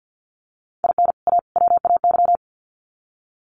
Morse code, heard as a keyed tone, about 50 wpm, 700 Hz. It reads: IN A WAY
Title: Sets of 3 Words 50wpm